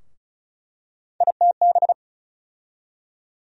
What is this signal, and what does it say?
Morse code, heard as a keyed tone, about 34 wpm, 700 Hz. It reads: ITB
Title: USA Call Sign Postfixes - 3 Letters 35wpm